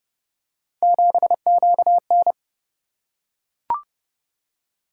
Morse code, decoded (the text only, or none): 7QN